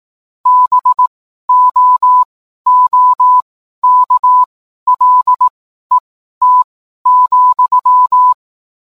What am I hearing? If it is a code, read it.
Morse code, heard as a keyed tone, about 18 wpm, spaced as for 12 wpm, 1000 Hz. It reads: BOOKLET,